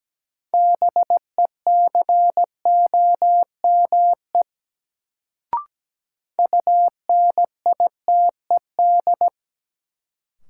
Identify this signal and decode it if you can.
Morse code, heard as a keyed tone, about 17 wpm, 700 Hz. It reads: BECOME UNITED
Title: Sentences from Top 600 Words - New Word Drill - 17wpm